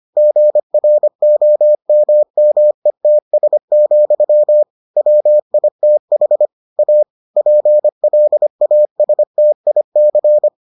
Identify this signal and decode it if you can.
Morse code, heard as a keyed tone, about 25 wpm, 600 Hz. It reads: GROMMETS, WITH A PLASTIC